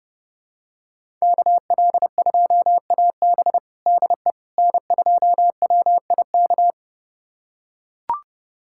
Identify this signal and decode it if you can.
Morse code, heard as a keyed tone, about 30 wpm, 700 Hz. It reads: KL2AB DE N2WIK
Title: USA Callsign DE Callsign - 5 Characters - 30wpm